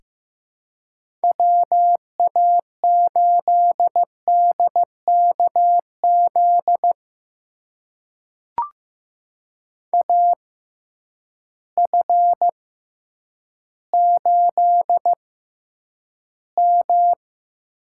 Morse code, heard as a keyed tone, about 15 wpm, spaced as for 9 wpm, 700 Hz. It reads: WA8DKZ AF8M
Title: USA Call Signs - 6x Character Spacing - 15wpm